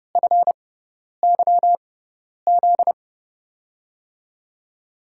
Morse code, heard as a keyed tone, about 30 wpm, spaced as for 10 wpm, 700 Hz. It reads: FYZ
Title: USA Call Signs - 6 Characters - 6x Character Spacing - 30wpm